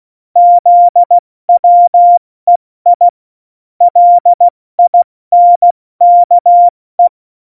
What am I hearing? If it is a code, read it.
Morse code, heard as a keyed tone, about 16 wpm, 700 Hz. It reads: ZWEI LINKE
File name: Morsekurs / Witze / Witz 24. November 2020, 70 F80